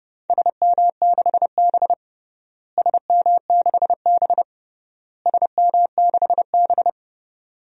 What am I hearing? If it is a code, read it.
Morse code, heard as a keyed tone, about 30 wpm, 700 Hz. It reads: SM6B SM6B SM6B
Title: DXCC-100 Call Signs - 4 Characters - Repeated 3x - 30wpm